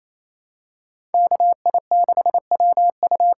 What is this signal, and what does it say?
Morse code, heard as a keyed tone, about 28 wpm, 700 Hz. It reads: KI6WU
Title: USA Call Signs - 6 Characters - 8x Character Spacing - 28wpm